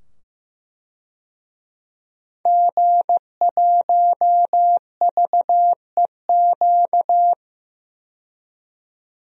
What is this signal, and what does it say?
Morse code, heard as a keyed tone, about 15 wpm, 700 Hz. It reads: G1VEQ
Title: DXCC-100 Call Signs - 5 Characters - 15wpm